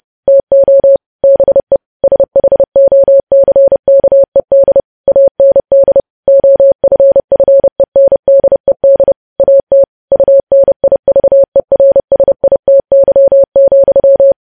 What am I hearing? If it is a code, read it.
Morse code, heard as a keyed tone, about 30 wpm, 550 Hz. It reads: TO BE SHOCKED AND OFFENDED AT UNIVERSITY,
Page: news headlines in morse code at 30 wpm